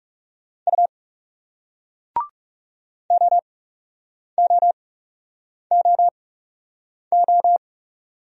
Morse code, heard as a keyed tone, about 40 wpm, spaced as for 8 wpm, 700 Hz. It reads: U OOOO